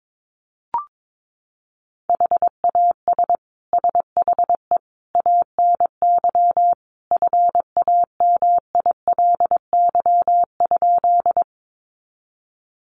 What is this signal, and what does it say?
Morse code, heard as a keyed tone, about 22 wpm, 700 Hz. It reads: HAS SHE ANY FAMILY?
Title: Sentences from Top 400 Words 22wpm